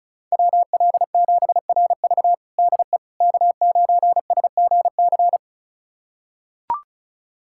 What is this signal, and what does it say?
Morse code, heard as a keyed tone, about 36 wpm, 700 Hz. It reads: WL7RV DE K9SGC
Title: USA Callsign DE Callsign - 5 Characters - 35wpm